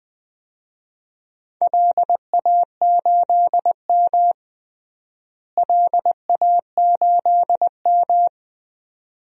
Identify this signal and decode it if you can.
Morse code, heard as a keyed tone, about 20 wpm, 700 Hz. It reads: LA8M LA8M
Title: DXCC-100 Call Signs - 4 Characters - Repeated 2x - 20wpm